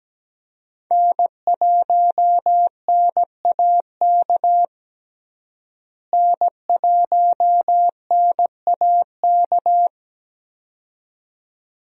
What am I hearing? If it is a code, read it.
Morse code, heard as a keyed tone, about 17 wpm, 700 Hz. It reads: N1NAK N1NAK
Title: USA Call Signs - 5 Characters - Repeated 2x - 17wpm